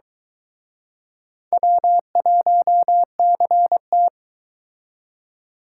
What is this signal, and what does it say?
Morse code, heard as a keyed tone, about 23 wpm, 700 Hz. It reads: W1CT